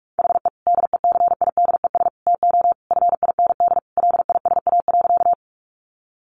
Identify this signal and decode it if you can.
Morse code, heard as a keyed tone, about 44 wpm, 700 Hz. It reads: HE DECIDES TO FIND LISA.
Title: Additional Easy Phrases 45wpm